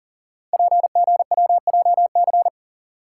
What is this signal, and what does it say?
Morse code, heard as a keyed tone, about 43 wpm, 700 Hz. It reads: PGWJC
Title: Code Groups - 5 Letters - 40wpm